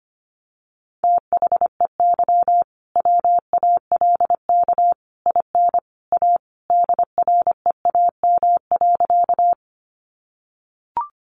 Morse code, heard as a keyed tone, about 25 wpm, 700 Hz. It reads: THEY WALK IN A DREAM.
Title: Sentences from Top 1500 Words 25wpm